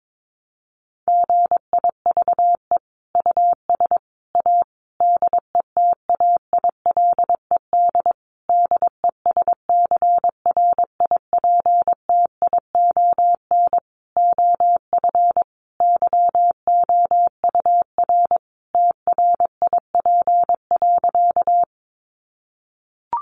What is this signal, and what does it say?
Morse code, heard as a keyed tone, about 22 wpm, 700 Hz. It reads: GIVE US A DETAILED DESCRIPTION OF YOUR TRIP.
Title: Sentences from Top 1500 Words 22wpm